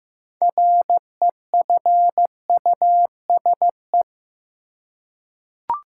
Morse code, heard as a keyed tone, about 15 wpm, 700 Hz. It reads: REFUSE
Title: Six Letter Words - 15wpm